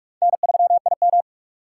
Morse code, heard as a keyed tone, about 46 wpm, 700 Hz. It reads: N3IM
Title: USA Call Signs - 3x Character Spacing - 45wpm